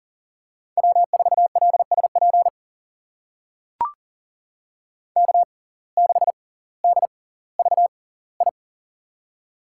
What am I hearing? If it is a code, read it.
Morse code, heard as a keyed tone, about 40 wpm, spaced as for 20 wpm, 700 Hz. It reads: W4LSP K6DVI
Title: USA Call Signs - 5 Characters - 6x Character Spacing - 40wpm